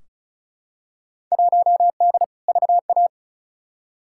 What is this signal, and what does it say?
Morse code, heard as a keyed tone, about 34 wpm, 700 Hz. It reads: 1D VA
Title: Field Day Exchanges - 35wpm